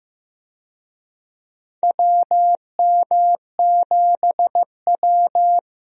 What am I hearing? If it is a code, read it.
Morse code, heard as a keyed tone, about 15 wpm, 700 Hz. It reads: WM7W
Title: USA Call Signs - 4 Characters - 15wpm